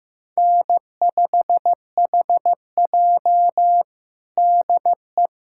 Morse code, heard as a keyed tone, about 16 wpm, 700 Hz. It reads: N5HJ DE
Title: USA Callsign DE Callsign - 4 Characters - 15wpm